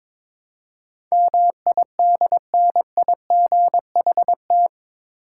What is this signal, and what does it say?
Morse code, heard as a keyed tone, about 22 wpm, 700 Hz. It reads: MIDNIGHT
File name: practice / 22wpm / Eight Letter Words - 22wpm